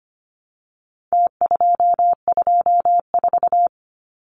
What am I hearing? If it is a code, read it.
Morse code, heard as a keyed tone, about 25 wpm, 700 Hz. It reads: T224
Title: Numbers - Contest Style - 4-Digit 25wpm